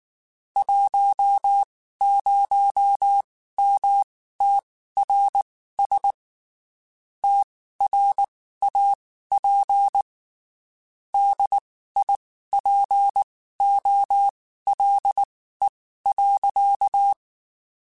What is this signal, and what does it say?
Morse code, heard as a keyed tone, about 19 wpm, spaced as for 14 wpm, 800 Hz. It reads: 10MTRS TRAP DIPOLE.
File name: QSOs Spaced